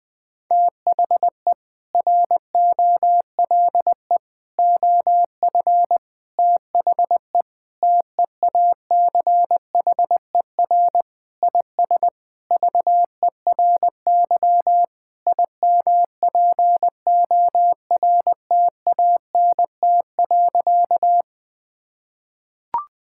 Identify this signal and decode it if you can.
Morse code, heard as a keyed tone, about 20 wpm, 700 Hz. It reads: THE ROLE OF THE TEACHER IS VERY IMPORTANT.
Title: Sentences with 4-Word Phrases - Spoken Beginning - 20wpm